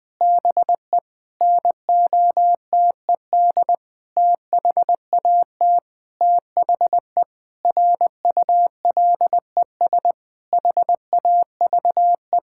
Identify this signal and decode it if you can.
Morse code, heard as a keyed tone, about 20 wpm, 700 Hz. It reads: BE NOTED THAT THE RULES HAVE
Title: Sentences with 5-Word Phrases - Spoken Beginning - 20wpm